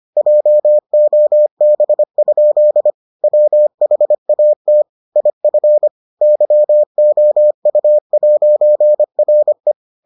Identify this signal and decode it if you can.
Morse code, heard as a keyed tone, about 25 wpm, 600 Hz. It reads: JOB? WHAT IF YOU'RE